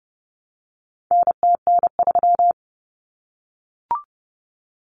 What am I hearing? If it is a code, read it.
Morse code, heard as a keyed tone, about 30 wpm, 700 Hz. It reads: NTN3
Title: Common Cut Numbers - 4-Digit 30wpm